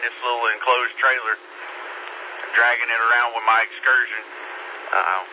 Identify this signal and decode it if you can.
Emergency Alert System (SAME) tones, heard as harsh buzzing bursts, with speech. It is heard from 0.9 to 4.3 s.